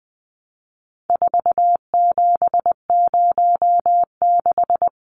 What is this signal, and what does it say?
Morse code, heard as a keyed tone, about 20 wpm, 700 Hz. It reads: 4706
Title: Numbers - 4-Digit 20wpm